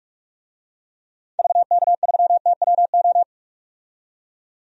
Morse code, heard as a keyed tone, about 42 wpm, 700 Hz. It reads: VK3TWO
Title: DXCC-100 Call Signs - 6 Characters - 45wpm